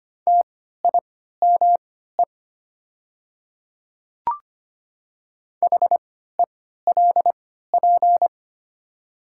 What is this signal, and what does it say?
Morse code, heard as a keyed tone, about 25 wpm, spaced as for 14 wpm, 690 Hz. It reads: TIME HELP
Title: 3x Character Spacing - Top 300 Words - 25wpm